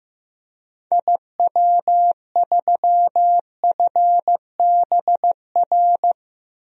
Morse code, heard as a keyed tone, about 15 wpm, 700 Hz. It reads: IW3FBR
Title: DXCC-100 Call Signs - 6 Characters - 15wpm